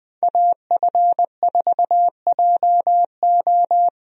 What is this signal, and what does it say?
Morse code, heard as a keyed tone, about 20 wpm, 700 Hz. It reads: AF4JO
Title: USA Call Signs - 5 Characters - Repeated 3x - 20wpm